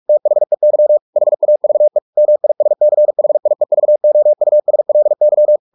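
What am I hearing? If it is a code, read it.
Morse code, heard as a keyed tone, about 44 wpm, 600 Hz. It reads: THEY HAVE MISCHIEVOUSLY
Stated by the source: Morse practice at 45 wpm, the novel Never by Ken Follett in English, from CW med Gnister.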